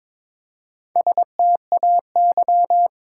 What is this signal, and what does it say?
Morse code, heard as a keyed tone, about 22 wpm, 700 Hz. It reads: STAY
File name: practice / 22wpm / Sentences with 2-Word Phrases - 22wpm